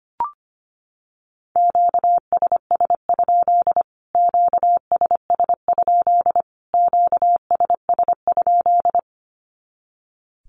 Morse code, heard as a keyed tone, about 25 wpm, 700 Hz. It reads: QSS? QSS? QSS?
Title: Q-Codes Repeated 3x 25wpm